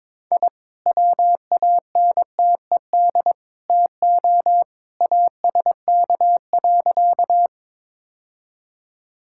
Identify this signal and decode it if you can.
Morse code, heard as a keyed tone, about 22 wpm, 700 Hz. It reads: I WANTED TO ASK.
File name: practice / 22wpm / Sentences with 3-Word Phrases - 22wpm